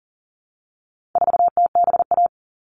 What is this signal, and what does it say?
Morse code, heard as a keyed tone, about 40 wpm, 700 Hz. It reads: 4TBA
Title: All Cut Numbers - 4-Digit 40wpm